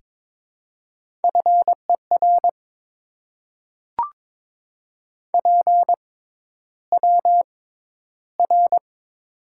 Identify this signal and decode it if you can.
Morse code, heard as a keyed tone, about 22 wpm, spaced as for 11 wpm, 700 Hz. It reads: FER PWR
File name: Top 90 QSO Elements - 6x Character Spacing - 22wpm